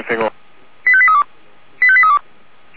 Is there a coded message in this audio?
Yes: five-tone selective call (ZVEI) from 0.9 to 2.2 s, heard as quick runs of five tones, tones of about 70 ms, with speech. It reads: 85832 85832